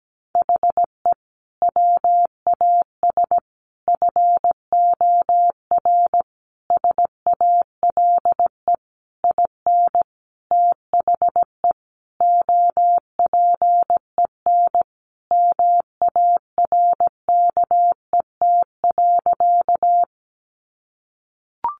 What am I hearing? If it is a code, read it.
Morse code, heard as a keyed tone, about 17 wpm, 700 Hz. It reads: HE WAS FOR SALE IN THE OPEN MARKET.
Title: Sentences from Top 1400 Words 17wpm